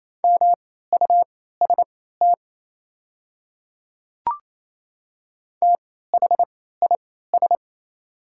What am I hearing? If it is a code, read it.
Morse code, heard as a keyed tone, about 28 wpm, spaced as for 16 wpm, 700 Hz. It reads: MUST THIS
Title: 3x Character Spacing - Top 200 Words - 28wpm